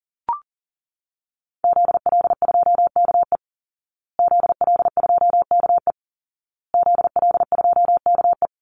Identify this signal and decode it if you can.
Morse code, heard as a keyed tone, about 40 wpm, 700 Hz. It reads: ZL2KE ZL2KE ZL2KE
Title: Call Signs Repeated 2x - 40wpm